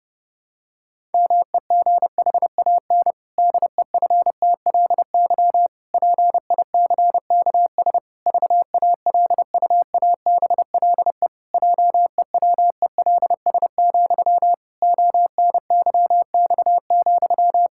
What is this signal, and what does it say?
Morse code, heard as a keyed tone, about 32 wpm, 700 Hz. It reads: MEGHAN DEFTLY PICKS VALUABLE JEWELS, ONYX,
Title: Pangram Challenge v2 - 30wpm